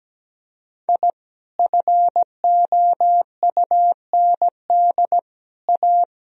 Morse code, heard as a keyed tone, about 17 wpm, 700 Hz. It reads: I FOUND A
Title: Sentences from Top 1000 Words - Review Part 1 - 17wpm